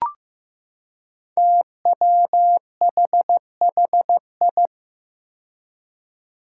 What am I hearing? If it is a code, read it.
Morse code, heard as a keyed tone, about 15 wpm, 700 Hz. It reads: TWHHI
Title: Five Challenging Letter-Number Combinations 15wpm